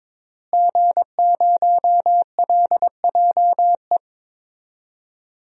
Morse code, heard as a keyed tone, about 22 wpm, 700 Hz. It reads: G0LJE
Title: DXCC-100 Call Signs - 5 Characters - Repeated 2x - 22wpm